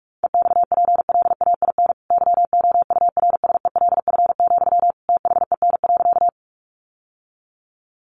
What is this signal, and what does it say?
Morse code, heard as a keyed tone, about 44 wpm, 700 Hz. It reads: EXPLAIN YOURSELF, THEN.
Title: Sentences from Top 1100 Words 45wpm